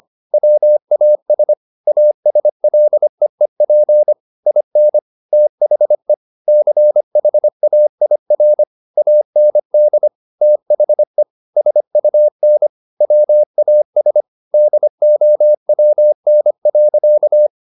Morse code, heard as a keyed tone, about 25 wpm, 600 Hz. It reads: WAS ASLEEP IN THE CHAIR AND THE SUN WAS DOWN.
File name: the old man and the sea part 1 25wpm